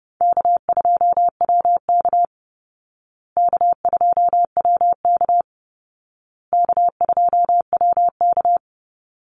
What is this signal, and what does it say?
Morse code, heard as a keyed tone, about 30 wpm, 700 Hz. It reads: K2WK K2WK K2WK
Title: Easy Call Signs Repeated 3x - 30wpm